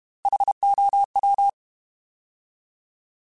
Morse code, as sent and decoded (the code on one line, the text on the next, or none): .... --- .--
HOW